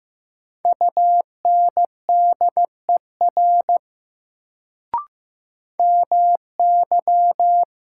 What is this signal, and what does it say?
Morse code, heard as a keyed tone, about 15 wpm, 700 Hz. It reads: UNDER MY
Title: Sets of 2 Words - New Word Drill - 15wpm